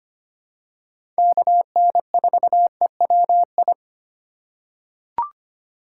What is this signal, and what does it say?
Morse code, heard as a keyed tone, about 25 wpm, 700 Hz. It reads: KN4EWI
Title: K1USN SST QSO Practice - 25wpm